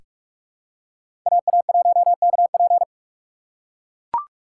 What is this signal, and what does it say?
Morse code, heard as a keyed tone, about 41 wpm, 700 Hz. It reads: AA1KP
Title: USA Call Signs - 6x Character Spacing - 45wpm